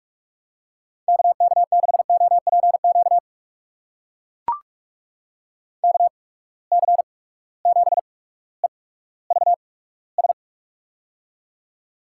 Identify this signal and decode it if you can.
Morse code, heard as a keyed tone, about 44 wpm, spaced as for 19 wpm, 700 Hz. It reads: KK6OPQ KC7EVS
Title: USA Call Signs - 6 Characters - 8x Character Spacing - 45wpm